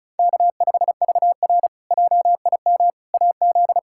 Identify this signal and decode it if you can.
Morse code, heard as a keyed tone, about 36 wpm, 700 Hz. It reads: K5VR JIM AZ
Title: K1USN SST QSO Practice - 35wpm